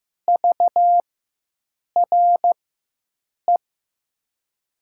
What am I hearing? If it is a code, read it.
Morse code, heard as a keyed tone, about 15 wpm, spaced as for 7 wpm, 700 Hz. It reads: VRE
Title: USA Call Signs - 4x Character Spacing - 15wpm